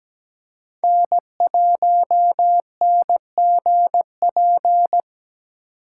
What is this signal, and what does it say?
Morse code, heard as a keyed tone, about 17 wpm, 700 Hz. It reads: N1NGP
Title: USA Call Signs - 5 Characters - 2x Character Spacing - 17wpm